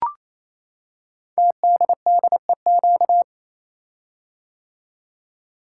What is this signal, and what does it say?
Morse code, heard as a keyed tone, about 27 wpm, 700 Hz. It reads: TDDEQ
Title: Code Groups - 5 Letters - 28wpm